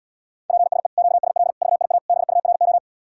Morse code, heard as a keyed tone, about 80 wpm, 700 Hz. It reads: PSE QSL VIA BURO